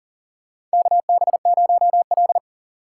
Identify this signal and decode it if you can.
Morse code, heard as a keyed tone, about 40 wpm, 700 Hz. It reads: KB0L
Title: USA Call Signs - 4 Characters - 6x Character Spacing - 40wpm